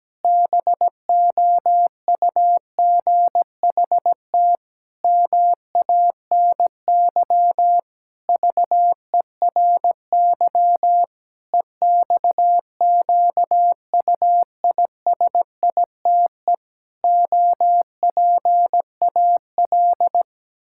Morse code, heard as a keyed tone, about 17 wpm, 700 Hz. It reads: BOUGHT MANY VERY EXQUISITE OPAL
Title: Pangram Challenge v2 - 17wpm